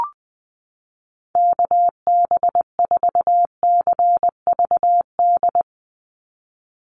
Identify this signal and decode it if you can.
Morse code, heard as a keyed tone, about 20 wpm, 700 Hz. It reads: KB4CVD
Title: USA Call Signs 20wpm